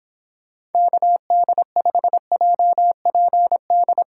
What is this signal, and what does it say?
Morse code, heard as a keyed tone, about 29 wpm, 700 Hz. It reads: KD5JPD